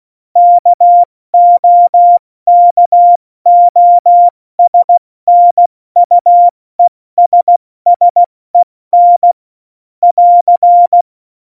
Morse code, heard as a keyed tone, about 16 wpm, spaced as for 14 wpm, 700 Hz. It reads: KOKOSNUESSEN <AR>